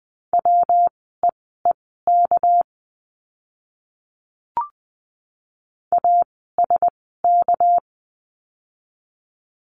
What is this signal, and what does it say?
Morse code, heard as a keyed tone, about 20 wpm, spaced as for 15 wpm, 700 Hz. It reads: WEEK ASK
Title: 199 Lesson 32f - Letter k - Words 20wpm